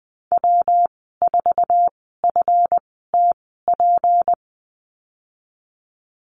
Morse code, heard as a keyed tone, about 20 wpm, spaced as for 15 wpm, 700 Hz. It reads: W4FTP
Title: 179 Lesson 29g - Slash - Call Signs 20wpm